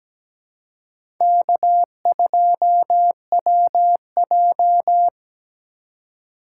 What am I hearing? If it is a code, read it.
Morse code, heard as a keyed tone, about 17 wpm, 700 Hz. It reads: K2WJ
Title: USA Call Signs - 4 Characters - 2x Character Spacing - 17wpm